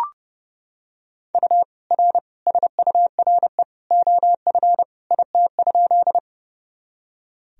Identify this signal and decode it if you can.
Morse code, heard as a keyed tone, about 30 wpm, 700 Hz. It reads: U R SURE OF IT?
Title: Sentences from Top 300 Words and Abbreviations 30wpm